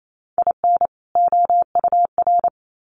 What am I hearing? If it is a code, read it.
Morse code, heard as a keyed tone, about 28 wpm, 700 Hz. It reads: IN OUR
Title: Sets of 2 Words 28wpm